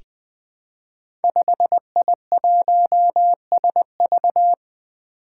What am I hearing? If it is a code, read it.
Morse code, heard as a keyed tone, about 20 wpm, 700 Hz. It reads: 5I1SV
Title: Five Challenging Letter-Number Combinations 20wpm